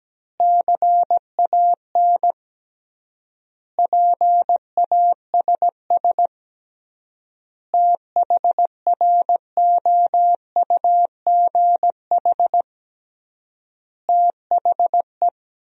Morse code, heard as a keyed tone, about 17 wpm, 700 Hz. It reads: CAN PASS THROUGH THE
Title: 3x Word-Spacing - Sentences from Top 400 Words - 17wpm